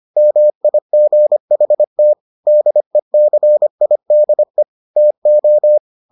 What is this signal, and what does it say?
Morse code, heard as a keyed tone, about 25 wpm, 600 Hz. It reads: MIGHT DECIDE TO